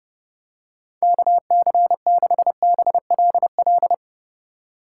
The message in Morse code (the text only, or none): KC6BLL